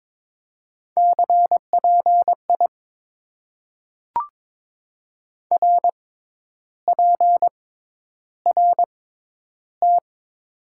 Morse code, heard as a keyed tone, about 22 wpm, spaced as for 10 wpm, 700 Hz. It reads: CPI RPRT
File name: Top 90 QSO Elements - 6x Character Spacing - 22wpm